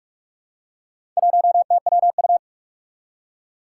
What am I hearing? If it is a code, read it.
Morse code, heard as a keyed tone, about 46 wpm, 700 Hz. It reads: 1TWU
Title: Four Challenging Letter-Number Combinations 45wpm